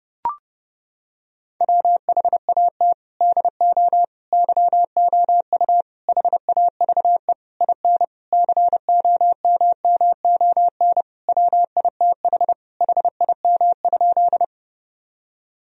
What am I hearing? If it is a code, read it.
Morse code, heard as a keyed tone, about 30 wpm, 700 Hz. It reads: WHAT DO YOU HAVE IN COMMON WITH HIM?
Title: Sentences from Top 1000 Words - Review Part 3 - 30wpm